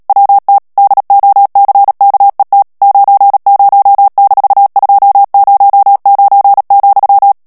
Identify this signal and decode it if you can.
Morse code, heard as a keyed tone, about 37 wpm, 800 Hz. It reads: WT DOCKET 90-209,
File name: morse mp3 / ANGLAIS texte clair 3 vitesse 35 mots minute